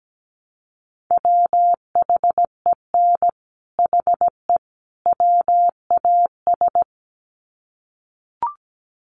Morse code, heard as a keyed tone, about 17 wpm, 700 Hz. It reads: WHEN HE WAS
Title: Sets of 3 Words 17wpm